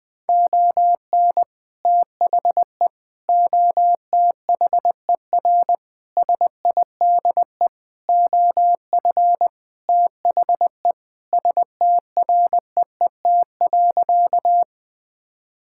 Morse code, heard as a keyed tone, about 20 wpm, 700 Hz. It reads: ON THE OTHER SIDE OF THE STREET.
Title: Sentences with 2-Word Phrases - 20wpm